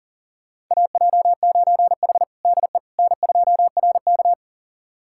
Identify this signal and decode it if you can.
Morse code, heard as a keyed tone, about 37 wpm, 700 Hz. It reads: AJ9H DE N2RK
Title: USA Callsign DE Callsign - 4 Characters - 40wpm